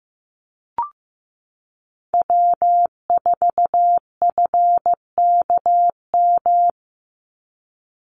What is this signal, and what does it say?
Morse code, heard as a keyed tone, about 15 wpm, 700 Hz. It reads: W4FKM
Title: Call Signs Repeated 3x - 15wpm